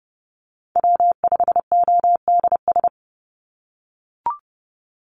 Morse code, heard as a keyed tone, about 31 wpm, 700 Hz. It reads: W5ODS
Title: NA SOTA Chaser Callsigns 30wpm